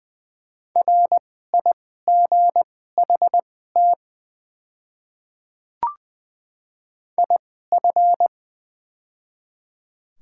Morse code, heard as a keyed tone, about 20 wpm, spaced as for 15 wpm, 700 Hz. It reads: RIGHT IF
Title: 2x Character Spacing - Top 200 Words - 20wpm